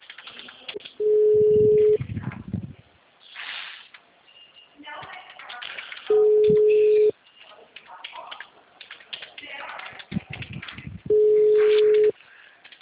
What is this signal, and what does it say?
Ringback tone (Europe, 425 Hz), with speech.
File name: out- -2023- - - .19544